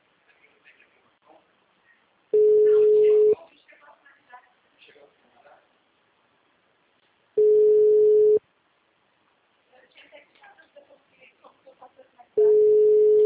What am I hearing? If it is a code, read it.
Ringback tone (Europe, 425 Hz), with speech.
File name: out- -2034- - - .5198